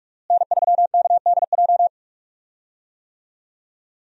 Morse code, heard as a keyed tone, about 42 wpm, 700 Hz. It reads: N3KDJ
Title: USA Call Signs - 5 Characters - Repeated 2x - 45wpm